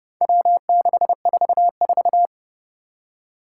Morse code, heard as a keyed tone, about 32 wpm, 700 Hz. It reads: W644
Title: Four Challenging Letter-Number Combinations 30wpm